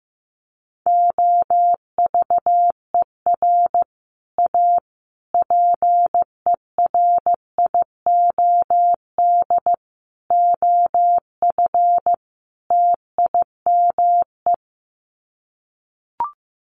Morse code, heard as a keyed tone, about 15 wpm, 700 Hz. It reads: OVER A PERIOD OF TIME
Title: Sets of 5 Words Encore2 15wpm